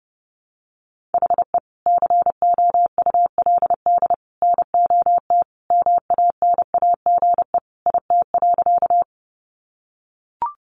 Morse code, heard as a keyed tone, about 30 wpm, 700 Hz. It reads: HE COULD NOT MANAGE IT.
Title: Sentences from Top 1400 Words 30wpm